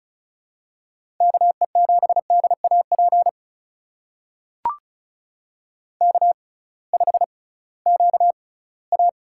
Morse code, heard as a keyed tone, about 34 wpm, spaced as for 20 wpm, 700 Hz. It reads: KE7DAP K5QA
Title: USA Call Signs - 6x Character Spacing - 35wpm